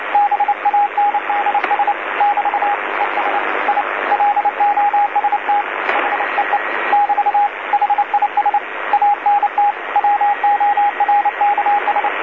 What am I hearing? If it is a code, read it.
Morse code, heard as a keyed tone, about 28 wpm, 820 Hz, with radio noise. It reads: BANDS <BT> ES I LOST HEE <BT> HIS ANT WORKS
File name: DC-TRCVR / 80-CW